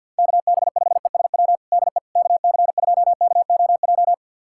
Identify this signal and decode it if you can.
Morse code, heard as a keyed tone, about 52 wpm, 700 Hz. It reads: KB5ESW DE KC2KOJ
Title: USA Callsign DE Callsign 50wpm